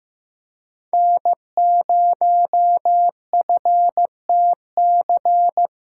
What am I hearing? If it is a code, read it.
Morse code, heard as a keyed tone, about 16 wpm, 700 Hz. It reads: N0FTC
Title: USA Callsign DE Callsign - 5 Characters - 15wpm